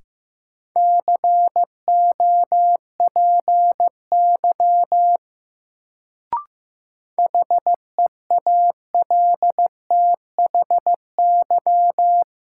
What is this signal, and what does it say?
Morse code, heard as a keyed tone, about 15 wpm, 700 Hz. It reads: COPY HEALTHY
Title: Sentences from Top 1400 Words - New Word Drill Alt - 15wpm